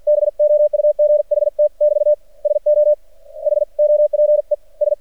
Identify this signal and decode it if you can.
Morse code, heard as a keyed tone, about 50 wpm, spaced as for 44 wpm, 600 Hz. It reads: BOUGHT <BT> SO HOPE S
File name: archived / CRSnet